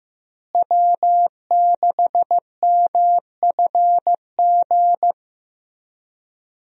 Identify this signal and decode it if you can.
Morse code, heard as a keyed tone, about 15 wpm, 700 Hz. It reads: W6MFG
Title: USA Call Signs - 5 Characters - Repeated 3x - 15wpm